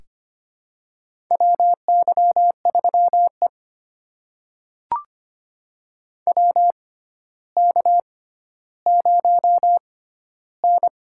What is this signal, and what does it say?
Morse code, heard as a keyed tone, about 25 wpm, spaced as for 13 wpm, 700 Hz. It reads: WY3E WK0N
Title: USA Call Signs - 4 Characters - 6x Character Spacing - 25wpm